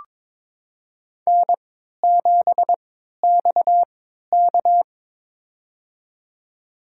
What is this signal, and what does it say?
Morse code, heard as a keyed tone, about 22 wpm, spaced as for 12 wpm, 700 Hz. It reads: N7XK
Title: USA Call Signs - 3x Character Spacing - 22wpm